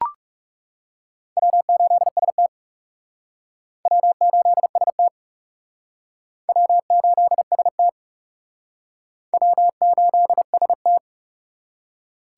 Morse code, heard as a keyed tone, about 37 wpm, 700 Hz. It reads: W8ST W8ST W8ST W8ST
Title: Speed Racing - USA Call Signs 30wpm